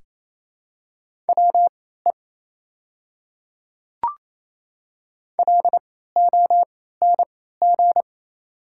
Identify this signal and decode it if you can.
Morse code, heard as a keyed tone, about 28 wpm, spaced as for 16 wpm, 690 Hz. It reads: WE LONG
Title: 3x Character Spacing - Top 100 Words - 28wpm